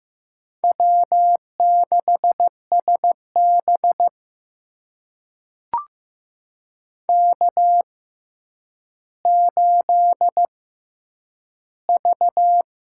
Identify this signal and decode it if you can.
Morse code, heard as a keyed tone, about 15 wpm, spaced as for 9 wpm, 700 Hz. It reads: W6SB K8V
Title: USA Call Signs - 4 Characters - 6x Character Spacing - 15wpm